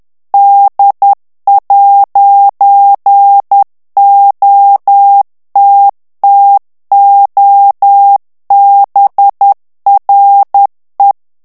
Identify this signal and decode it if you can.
Morse code, heard as a keyed tone, about 10 wpm, 800 Hz. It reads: D'OTTOBRE
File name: morse mp3 / ITALIEN texte clair 1 vitesse 10 mots minute